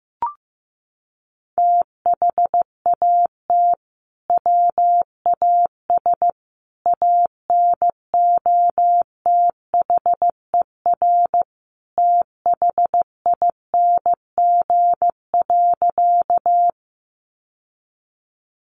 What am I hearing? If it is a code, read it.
Morse code, heard as a keyed tone, about 15 wpm, 700 Hz. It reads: THAT WAS ANOTHER THING.
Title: Sentences from Top 600 Words 15wpm